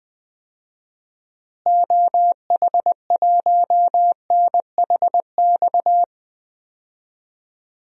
Morse code, heard as a keyed tone, about 20 wpm, 700 Hz. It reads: OH1NHX
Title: DXCC-100 Call Signs - 6 Characters - Repeated 3x - 20wpm